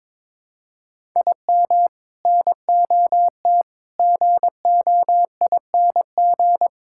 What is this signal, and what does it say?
Morse code, heard as a keyed tone, about 22 wpm, 700 Hz. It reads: IM NOT GOING
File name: practice / 22wpm / Sentences with 3-Word Phrases - 22wpm